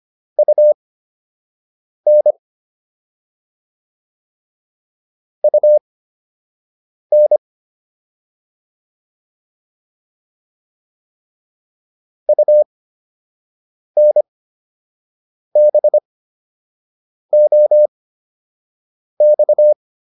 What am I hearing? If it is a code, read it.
Morse code, heard as a keyed tone, about 25 wpm, spaced as for 6 wpm, 600 Hz. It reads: UN UN UNBOX
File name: UN5 06